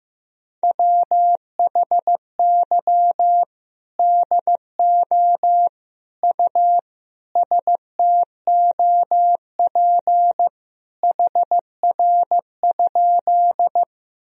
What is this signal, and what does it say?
Morse code, heard as a keyed tone, about 15 wpm, 700 Hz. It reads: WHY DO U STOP HR?
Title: Sentences from Top 300 Words and Abbreviations 15wpm